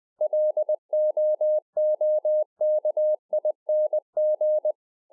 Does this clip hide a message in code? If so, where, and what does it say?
Yes: Morse code from 0.2 to 4.7 s, heard as a keyed tone, about 20 wpm, 610 Hz. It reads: LOOKING